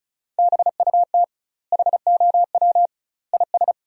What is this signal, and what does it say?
Morse code, heard as a keyed tone, about 34 wpm, 700 Hz. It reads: BUT HOW IS
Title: 2x Word-Spacing - Sentences from Top 100 Words - 35wpm